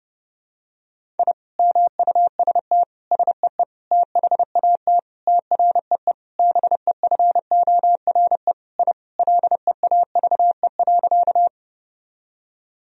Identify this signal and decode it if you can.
Morse code, heard as a keyed tone, about 30 wpm, 700 Hz. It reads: I MUST SEE THAT TREE BEFORE I LEAVE.
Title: Sentences from Top 1000 Words - Review Part 1 - 30wpm